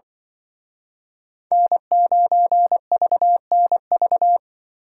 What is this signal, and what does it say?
Morse code, heard as a keyed tone, about 24 wpm, 700 Hz. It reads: N9VNV
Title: Speed Racing - USA 1x3 Call Signs 20wpm